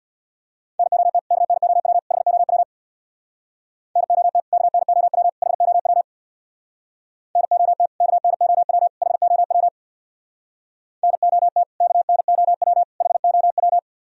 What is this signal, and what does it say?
Morse code, heard as a keyed tone, about 64 wpm, 700 Hz. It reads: NOT KNOW HOW NOT KNOW HOW NOT KNOW HOW NOT KNOW HOW